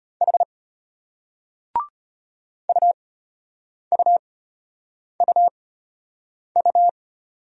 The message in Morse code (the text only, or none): H UUUU